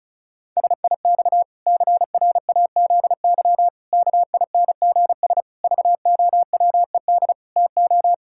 Morse code, heard as a keyed tone, about 36 wpm, 700 Hz. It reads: SIX CRAZY KINGS VOWED TO
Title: Pangram Challenge v2 - 35wpm